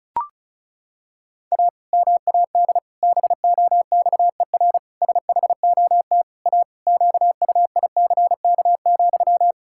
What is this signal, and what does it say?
Morse code, heard as a keyed tone, about 36 wpm, 700 Hz. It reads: A MAD BOXER SHOT A QUICK,
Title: Pangram Challenge v2 - 35wpm